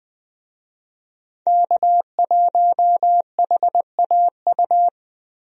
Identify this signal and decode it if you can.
Morse code, heard as a keyed tone, about 21 wpm, 700 Hz. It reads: K1HAU
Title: USA Call Signs - 5 Characters - 20wpm